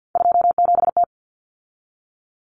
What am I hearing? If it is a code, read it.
Morse code, heard as a keyed tone, about 50 wpm, 700 Hz. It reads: 27T
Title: Common Cut Numbers - 3-Digit 50wpm